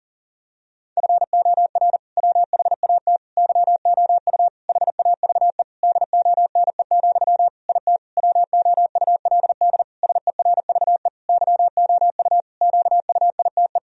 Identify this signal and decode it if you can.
Morse code, heard as a keyed tone, about 37 wpm, 700 Hz. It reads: FOR WHAT YOU HAVE DONE, IT WOULD SERVE YOU QUITE
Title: Sentences from Top 1000 Words - Review Part 2 - 40wpm